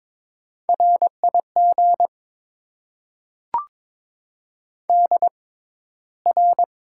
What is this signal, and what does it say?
Morse code, heard as a keyed tone, about 22 wpm, spaced as for 14 wpm, 700 Hz. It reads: RIG DR
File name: Top 90 QSO Elements - 6x Character Spacing - 22wpm